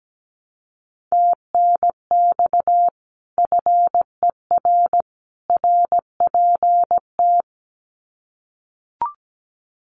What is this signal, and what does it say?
Morse code, heard as a keyed tone, about 17 wpm, 700 Hz. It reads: TNX FER RPT
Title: Common QSO Phrases 17wpm